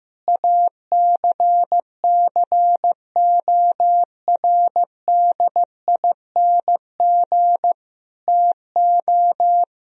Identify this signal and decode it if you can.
Morse code, heard as a keyed tone, about 16 wpm, 700 Hz. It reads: ACCORDING TO
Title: Sentences from Top 1000 Words - Review Part 4 - 15wpm